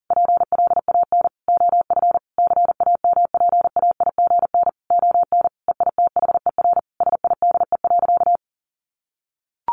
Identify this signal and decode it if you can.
Morse code, heard as a keyed tone, about 40 wpm, 700 Hz. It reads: PLAN OF CAMPAIGN ON EITHER SIDE.
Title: Sentences from Top 1200 Words 40wpm